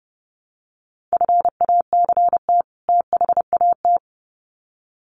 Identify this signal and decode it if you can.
Morse code, heard as a keyed tone, about 30 wpm, 700 Hz. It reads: FACT THAT
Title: Sets of 2 Words 30wpm